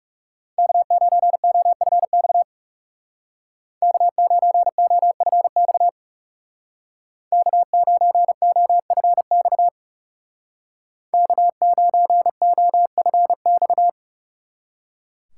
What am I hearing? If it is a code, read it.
Morse code, heard as a keyed tone, about 35 wpm, 700 Hz. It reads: K9OFX K9OFX K9OFX K9OFX
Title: Speed Racing - USA 1x3 Call Signs 30wpm